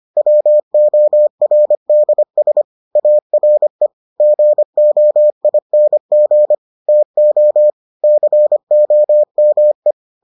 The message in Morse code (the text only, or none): WORDS ARE GOING TO COME